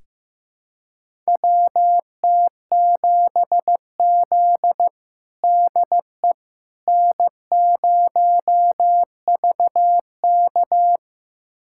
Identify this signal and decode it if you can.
Morse code, heard as a keyed tone, about 15 wpm, 700 Hz. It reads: WT7Z DE N0VK
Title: USA Callsign DE Callsign - 4 Characters - 15wpm